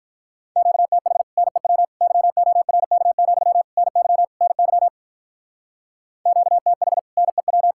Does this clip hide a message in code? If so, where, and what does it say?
Yes: Morse code from 0.6 to 7.7 s, heard as a keyed tone, about 45 wpm, 700 Hz. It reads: QTH NEW YORK, NY NY QTH NEW